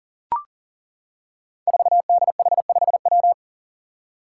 Morse code, heard as a keyed tone, about 37 wpm, 700 Hz. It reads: 4DH5W
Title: Five Challenging Letter-Number Combinations 40wpm